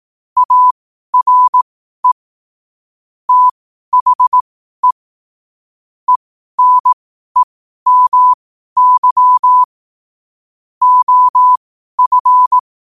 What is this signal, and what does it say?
Morse code, heard as a keyed tone, about 18 wpm, spaced as for 12 wpm, 1000 Hz. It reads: ARE THE ENEMY OF
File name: qotdfiles / 12wpm